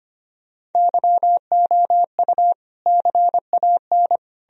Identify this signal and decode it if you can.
Morse code, heard as a keyed tone, about 33 wpm, 700 Hz. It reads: YOU CAN